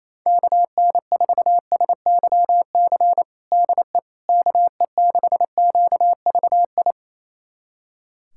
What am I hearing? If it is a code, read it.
Morse code, heard as a keyed tone, about 28 wpm, 700 Hz. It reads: KN4SYC DE KE6QVI
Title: USA Callsign DE Callsign - 6 Characters - 28wpm